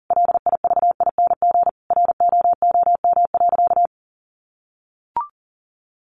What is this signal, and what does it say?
Morse code, heard as a keyed tone, about 40 wpm, 700 Hz. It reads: LIVING ROOM.
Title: Sentences from Top 1200 Words 40wpm